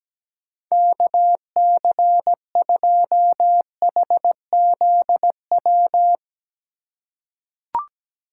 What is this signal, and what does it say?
Morse code, heard as a keyed tone, about 17 wpm, 700 Hz. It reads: KC2HZW
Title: USA Call Signs - 6 Characters - 3x Character Spacing - 17wpm